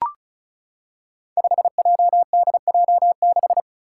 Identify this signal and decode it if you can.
Morse code, heard as a keyed tone, about 36 wpm, 700 Hz. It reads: 5JDJ6
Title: Five Challenging Letter-Number Combinations 35wpm